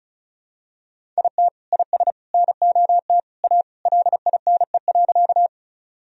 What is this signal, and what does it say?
Morse code, heard as a keyed tone, about 36 wpm, 700 Hz. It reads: IT IS NOT A LINE.
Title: Sentences from Top 1000 Words - Review Part 1 - 35wpm